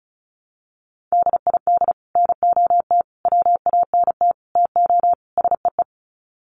Morse code, heard as a keyed tone, about 34 wpm, 700 Hz. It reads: DID NOT WANT TO SEE
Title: Sets of 5 Words Encore2 35wpm